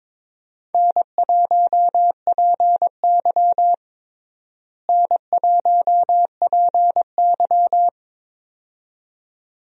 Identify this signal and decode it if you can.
Morse code, heard as a keyed tone, about 22 wpm, 700 Hz. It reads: N1PY N1PY
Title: USA Call Signs - 4 Characters - Repeated 2x - 22wpm